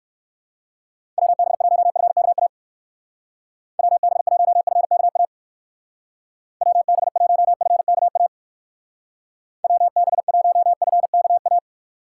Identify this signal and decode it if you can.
Morse code, heard as a keyed tone, about 66 wpm, 700 Hz. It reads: WB1FKA WB1FKA WB1FKA WB1FKA